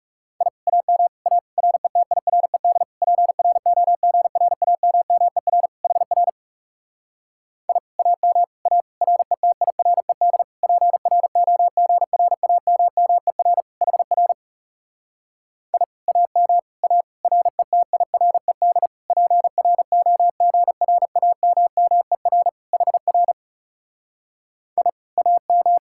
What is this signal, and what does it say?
Morse code, heard as a keyed tone, about 40 wpm, 700 Hz. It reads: I AM A RETIRED PROGRAMMER HR I AM A RETIRED PROGRAMMER HR I AM A RETIRED PROGRAMMER HR I AM